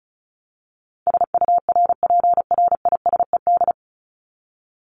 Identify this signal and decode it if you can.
Morse code, heard as a keyed tone, about 34 wpm, 700 Hz. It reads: SURPRISED
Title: Top 1501-1750 Words 35wpm